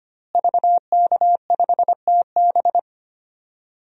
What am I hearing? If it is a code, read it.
Morse code, heard as a keyed tone, about 25 wpm, 700 Hz. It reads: VK5TB
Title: DXCC-100 Call Signs - 5 Characters - Repeated 3x - 25wpm